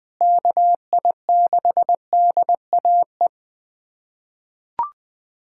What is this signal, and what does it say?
Morse code, heard as a keyed tone, about 20 wpm, 700 Hz. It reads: KI6DAE
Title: USA Callsign DE Callsign 20wpm